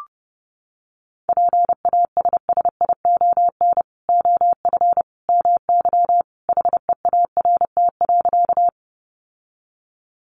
Morse code, heard as a keyed tone, about 30 wpm, 700 Hz. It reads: PASSION OF MY HEART.
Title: Sentences from Top 1300 Words 30wpm